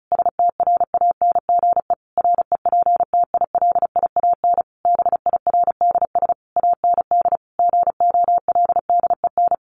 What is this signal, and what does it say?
Morse code, heard as a keyed tone, about 34 wpm, 700 Hz. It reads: STRANGE REPTILIAN BIRDS AND GOLDEN
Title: A Sound Of Thunder 35wpm